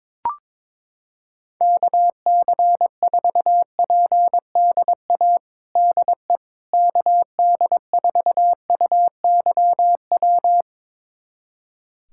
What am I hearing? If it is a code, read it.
Morse code, heard as a keyed tone, about 22 wpm, 700 Hz. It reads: KC4PDA DE KD4UYW
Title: USA Callsign DE Callsign - 6 Characters - 22wpm